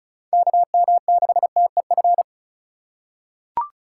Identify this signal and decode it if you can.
Morse code, heard as a keyed tone, about 35 wpm, 700 Hz. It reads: KM6TEF
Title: USA Call Signs - 6 Characters - 3x Character Spacing - 35wpm